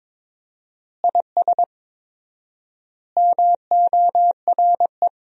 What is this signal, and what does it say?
Morse code, heard as a keyed tone, about 22 wpm, 700 Hz. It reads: IS MORE
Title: 4x Word-Spacing - Sentences from Top 500 Words - 22wpm